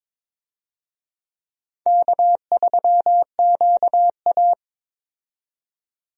Morse code, heard as a keyed tone, about 22 wpm, 700 Hz. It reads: K3QA
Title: USA Call Signs - 4 Characters - 22wpm